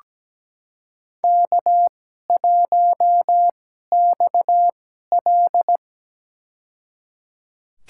Morse code, heard as a keyed tone, about 17 wpm, spaced as for 12 wpm, 700 Hz. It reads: K1XL
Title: USA Call Signs - 4 Characters - 2x Character Spacing - 17wpm